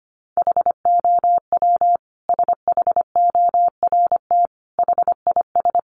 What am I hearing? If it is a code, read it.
Morse code, heard as a keyed tone, about 25 wpm, 700 Hz. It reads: HOW SHORT HIS
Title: Sentences from Top 400 Words 25wpm